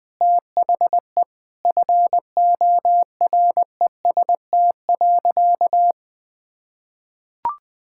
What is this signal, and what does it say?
Morse code, heard as a keyed tone, about 20 wpm, 700 Hz. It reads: THE FOREST.
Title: Sentences with 5-Word Phrases - Spoken Beginning - 20wpm